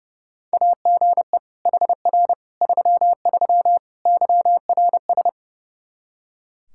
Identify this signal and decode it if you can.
Morse code, heard as a keyed tone, about 30 wpm, 700 Hz. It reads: AGE HR 33 YRS
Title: K1USN SST QSO Practice 30wpm